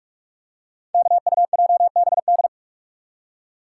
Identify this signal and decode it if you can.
Morse code, heard as a keyed tone, about 46 wpm, 700 Hz. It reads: KUJBD
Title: Code Groups - 5 Letter:Numbers - 45wpm